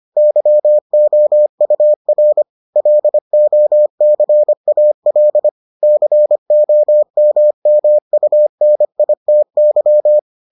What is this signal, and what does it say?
Morse code, heard as a keyed tone, about 25 wpm, 600 Hz. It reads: YOUR LOCAL COMMUNITY